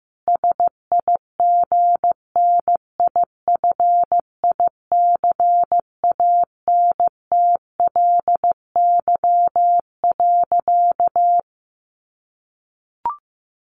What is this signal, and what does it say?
Morse code, heard as a keyed tone, about 15 wpm, 700 Hz. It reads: SIGNIFICANTLY.
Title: Sentences from Top 1500 Words 15wpm